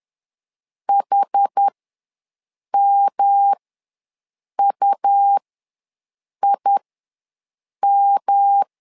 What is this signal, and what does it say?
Morse code, heard as a keyed tone, about 10 wpm, spaced as for 6 wpm, 800 Hz. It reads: HMUIM
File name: LEZIONI / 3 MWRO / 3 M ws 10wpm 2121